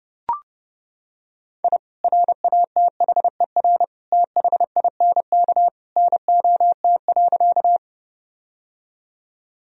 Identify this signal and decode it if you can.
Morse code, heard as a keyed tone, about 30 wpm, 700 Hz. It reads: I RATHER THINK NOT.
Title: Sentences from Top 1000 Words - Review Part 4 - 30wpm